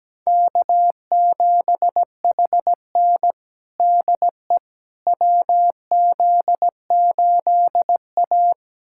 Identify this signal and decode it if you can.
Morse code, heard as a keyed tone, about 17 wpm, 700 Hz. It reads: K7HN DE WZ8A
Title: USA Callsign DE Callsign - 4 Characters - 17wpm